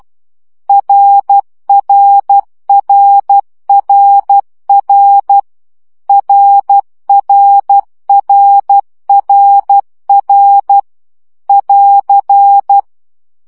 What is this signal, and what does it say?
Morse code, heard as a keyed tone, about 12 wpm, 800 Hz. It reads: RRRRR RRRRR <AR>
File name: CW gyakorlas r 10 baud